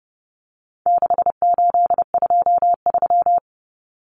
Morse code, heard as a keyed tone, about 30 wpm, 700 Hz. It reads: 6823
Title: Numbers - 4-Digit 30wpm